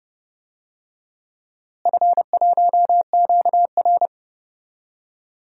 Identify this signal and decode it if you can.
Morse code, heard as a keyed tone, about 31 wpm, 700 Hz. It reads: F1QR
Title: DXCC-100 Call Signs - 4 Characters - Repeated 3x - 30wpm